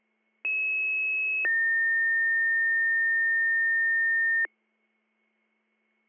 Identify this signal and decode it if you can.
Two-tone page, heard as a one-second tone, then a three-second tone, about 2600 then 1800 Hz. It is heard from 0.4 to 4.5 s.